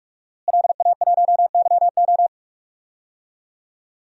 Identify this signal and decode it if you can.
Morse code, heard as a keyed tone, about 44 wpm, 700 Hz. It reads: LA1YO
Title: DXCC-100 Call Signs - 5 Characters - Repeated 3x - 45wpm